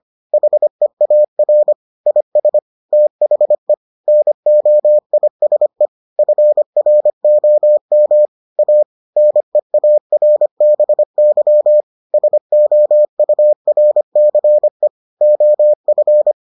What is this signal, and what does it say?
Morse code, heard as a keyed tone, about 25 wpm, 600 Hz. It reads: HEAR IS THE NOISE FROM A NEARBY SOURCE OF